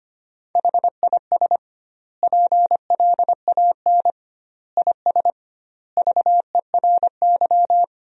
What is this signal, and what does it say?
Morse code, heard as a keyed tone, about 25 wpm, 700 Hz. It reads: HIS PLAN IS VERY
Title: 2x Word-Spacing - Sentences from Top 500 Words - 25wpm